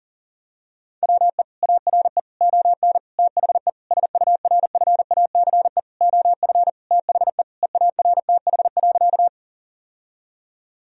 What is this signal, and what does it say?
Morse code, heard as a keyed tone, about 40 wpm, 700 Hz. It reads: WE ARE ON THE SURFACE OF THE EARTH.
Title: Sentences from Top 1000 Words - Review Part 3 - 40wpm